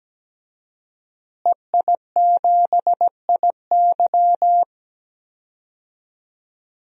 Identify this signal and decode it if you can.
Morse code, heard as a keyed tone, about 17 wpm, 700 Hz. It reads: EI7IY
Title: DXCC-100 Call Signs - 5 Characters - Repeated 3x - 17wpm